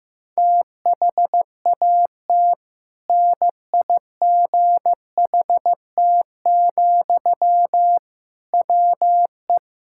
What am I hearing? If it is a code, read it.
Morse code, heard as a keyed tone, about 15 wpm, 700 Hz. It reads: THAT NIGHT, WE
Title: Sentences from Top 1000 Words - Review Part 6 - 15wpm